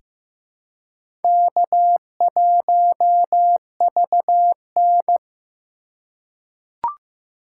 Morse code, heard as a keyed tone, about 15 wpm, 700 Hz. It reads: K1VN
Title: USA Call Signs - 3x Character Spacing - 15wpm